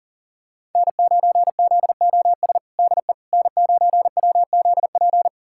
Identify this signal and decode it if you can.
Morse code, heard as a keyed tone, about 40 wpm, 700 Hz. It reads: N9ZOS DE N9WZP
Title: USA Callsign DE Callsign - 5 Characters - 40wpm